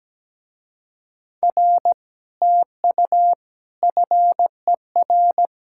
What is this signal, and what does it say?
Morse code, heard as a keyed tone, about 17 wpm, 700 Hz. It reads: R TU FER
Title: Speed-Racing Common QSO Phrases - 17wpm